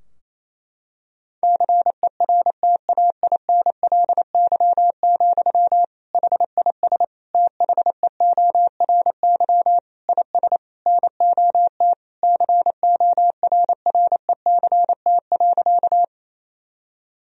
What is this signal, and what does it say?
Morse code, heard as a keyed tone, about 28 wpm, 700 Hz. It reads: CERTAINLY, HIS THEORY IS NOT CORRECT.
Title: Sentences from Top 1000 Words - Review Part 3 - 28wpm